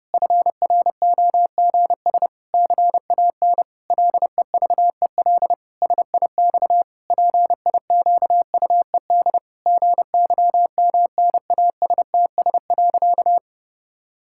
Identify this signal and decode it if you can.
Morse code, heard as a keyed tone, about 30 wpm, 700 Hz. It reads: FROGS CAN LEVEL SIX PIQUED GYMNASTS.
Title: Pangram Challenge v2 - 30wpm